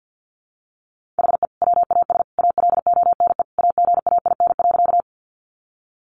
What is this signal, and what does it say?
Morse code, heard as a keyed tone, about 50 wpm, 700 Hz. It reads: HE WAS ALONE AGAIN.
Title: Sentences from Top 1000 Words 50wpm